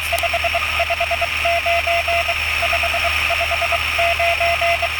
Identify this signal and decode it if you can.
Morse code, heard as a keyed tone, about 22 wpm, 680 Hz, with radio noise. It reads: 559 559